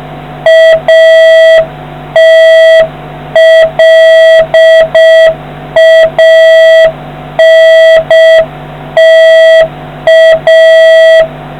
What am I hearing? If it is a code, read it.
Morse code, heard as a keyed tone, about 6 wpm, 650 Hz, with noise. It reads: ATLANTA